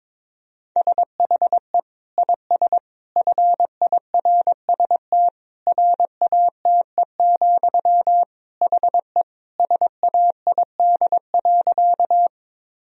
Morse code, heard as a keyed tone, about 22 wpm, 700 Hz. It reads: SHE IS FIRST RATE, HE SAID.
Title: Sentences from Top 1000 Words - Review Part 4 - 22wpm